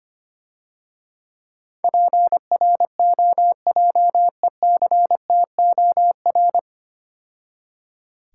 Morse code, heard as a keyed tone, about 25 wpm, 700 Hz. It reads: PROJECTOR
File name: Easy Words - Speed-Racing with Hints - 25wpm